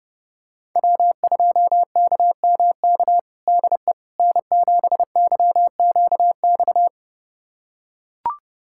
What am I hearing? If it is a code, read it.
Morse code, heard as a keyed tone, about 32 wpm, 700 Hz. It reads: W2KMK DE N7YQX
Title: USA Callsign DE Callsign - 5 Characters - 30wpm